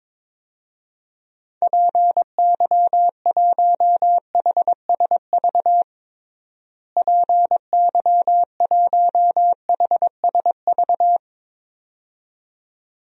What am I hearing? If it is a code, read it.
Morse code, heard as a keyed tone, about 22 wpm, 700 Hz. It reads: PY1HSV PY1HSV
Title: DXCC-100 Call Signs - 6 Characters - Repeated 2x - 22wpm